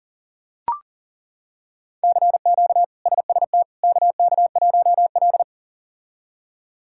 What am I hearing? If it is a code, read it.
Morse code, heard as a keyed tone, about 40 wpm, 700 Hz. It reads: CQ SST KK1L
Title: K1USN SST QSO Practice - 40wpm